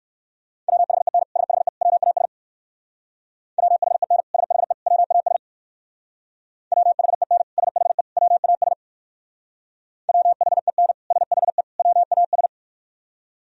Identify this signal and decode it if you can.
Morse code, heard as a keyed tone, about 57 wpm, 700 Hz. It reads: WHEN SHE WAS WHEN SHE WAS WHEN SHE WAS WHEN SHE WAS